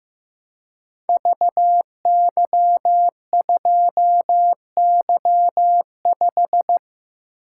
Morse code, heard as a keyed tone, about 15 wpm, 700 Hz. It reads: VY2Y5
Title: Code Groups - 5 Letter:Numbers - 15wpm